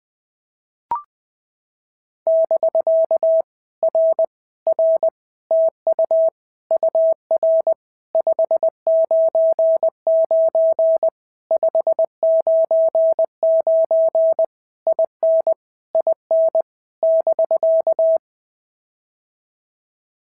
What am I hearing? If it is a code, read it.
Morse code, heard as a keyed tone, about 20 wpm, 650 Hz. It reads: <BK> R R TU UR 599 599 IN IN <BK>
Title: POTA QSO Practice - 20wpm